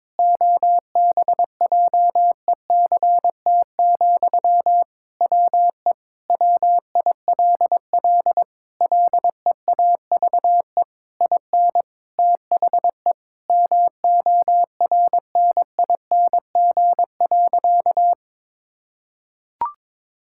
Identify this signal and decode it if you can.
Morse code, heard as a keyed tone, about 22 wpm, 700 Hz. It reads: OBJECT, WE WILL LEAVE IN THE MORNING.
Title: Sentences from Top 1000 Words - Review Part 6 - 22wpm